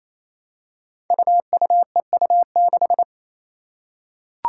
Morse code, heard as a keyed tone, about 28 wpm, 700 Hz. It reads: UUEU6
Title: Five Challenging Letter-Number Combinations 28wpm